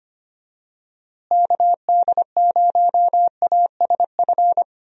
Morse code, heard as a keyed tone, about 25 wpm, 700 Hz. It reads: KD0ASF
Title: USA Call Signs - 6 Characters - Repeated 3x - 25wpm